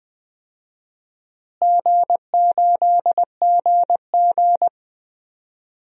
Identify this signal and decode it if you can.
Morse code, heard as a keyed tone, about 20 wpm, 700 Hz. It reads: G8GG
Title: DXCC-100 Call Signs - 4 Characters - Repeated 2x - 20wpm